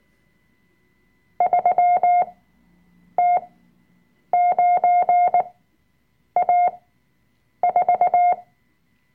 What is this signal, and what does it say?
Morse code, heard as a keyed tone, about 19 wpm, spaced as for 7 wpm, 710 Hz. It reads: 3T9A4
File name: Workshop Dateien / CW-Online-Kurs / Lektion 005